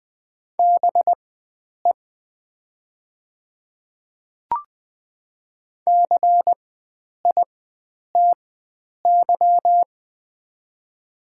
Morse code, heard as a keyed tone, about 20 wpm, spaced as for 9 wpm, 700 Hz. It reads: BE CITY
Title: 4x Character Spacing - Top 300 Words - 20wpm